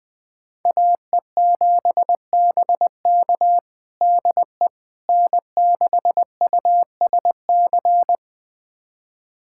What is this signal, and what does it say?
Morse code, heard as a keyed tone, about 20 wpm, 700 Hz. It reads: AE7BK DE N6USC
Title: USA Callsign DE Callsign - 5 Characters - 20wpm